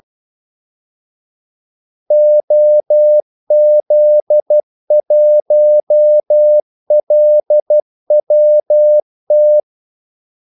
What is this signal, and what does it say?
Morse code, heard as a keyed tone, about 12 wpm, 600 Hz. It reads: OZ1LWT
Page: 2 ekstra ordmellemrum mellem kaldesignalerne